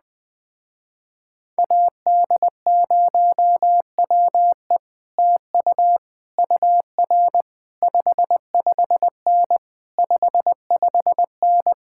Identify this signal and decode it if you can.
Morse code, heard as a keyed tone, about 20 wpm, 700 Hz. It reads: AD0WE TU UR 55N 55N